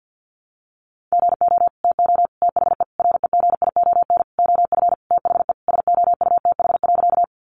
Morse code, heard as a keyed tone, about 50 wpm, 700 Hz. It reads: GO TO THE REGION OF THE SOUTH.
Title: Sentences from Top 1500 Words 50wpm